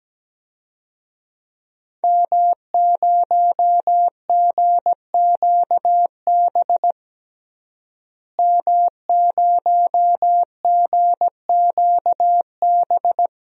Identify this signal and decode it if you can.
Morse code, heard as a keyed tone, about 17 wpm, 700 Hz. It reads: M0GQB M0GQB
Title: DXCC-100 Call Signs - 5 Characters - Repeated 3x - 17wpm